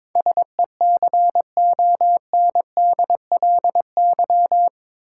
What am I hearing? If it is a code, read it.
Morse code, heard as a keyed tone, about 22 wpm, 700 Hz. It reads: SECONDLY
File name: practice / 22wpm / Eight Letter Words - 22wpm